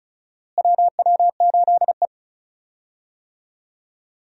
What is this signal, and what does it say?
Morse code, heard as a keyed tone, about 36 wpm, 700 Hz. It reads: WW8E
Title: USA Call Signs - 4 Characters - Repeated 3x - 35wpm